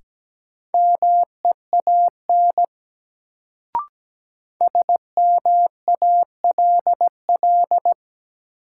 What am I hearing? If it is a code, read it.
Morse code, heard as a keyed tone, about 17 wpm, 700 Hz. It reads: MEAN SMALL
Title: Sentences from Top 200 Words - New Word Drill - 17wpm